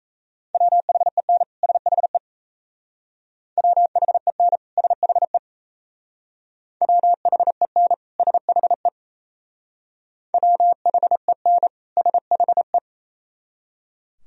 Morse code, heard as a keyed tone, about 35 wpm, 700 Hz. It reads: WHEN SHE WHEN SHE WHEN SHE WHEN SHE
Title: Speed-Racing - Sets of 2 Words 28wpm